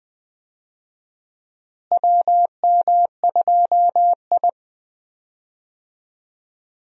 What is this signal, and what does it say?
Morse code, heard as a keyed tone, about 20 wpm, 700 Hz. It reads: WM2I
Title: USA Call Signs - 4 Characters - Repeated 2x - 20wpm